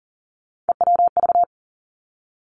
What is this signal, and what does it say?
Morse code, heard as a keyed tone, about 40 wpm, 700 Hz. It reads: EWV
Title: Three Challenging Letter-Number Combinations 40wpm